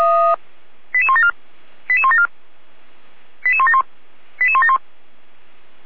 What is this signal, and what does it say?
Five-tone selective call (ZVEI), heard as quick runs of five tones, tones of about 70 ms. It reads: 88174 88174 88171 88171